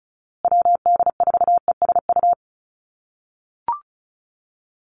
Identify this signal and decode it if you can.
Morse code, heard as a keyed tone, about 34 wpm, 700 Hz. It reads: WD4ESU
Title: USA Call Signs Repeated 3x 35wpm